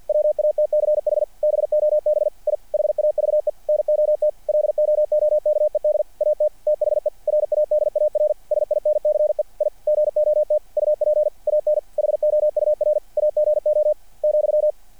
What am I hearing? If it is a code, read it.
Morse code, heard as a keyed tone, about 48 wpm, 600 Hz. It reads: WATCH BOB I HAVE NOT LOOKED AT THE RADAR SINCE I GOT UP AN HOUR AGO ,
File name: archived / CRSnet